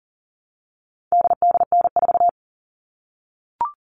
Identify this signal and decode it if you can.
Morse code, heard as a keyed tone, about 40 wpm, 700 Hz. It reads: DDN4
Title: All Cut Numbers - 4-Digit 40wpm